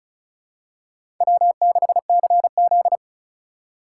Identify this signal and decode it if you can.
Morse code, heard as a keyed tone, about 36 wpm, 700 Hz. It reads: W6CZ
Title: USA Call Signs - 4 Characters - 4x Character Spacing - 35wpm